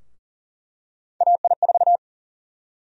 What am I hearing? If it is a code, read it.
Morse code, heard as a keyed tone, about 40 wpm, 700 Hz. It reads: AI4
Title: USA Call Sign Prefixes - Repeated 2x - 40wpm